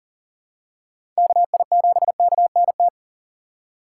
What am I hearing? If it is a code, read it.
Morse code, heard as a keyed tone, about 40 wpm, 700 Hz. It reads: KI7KNT
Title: USA Call Signs - 6 Characters - 3x Character Spacing - 40wpm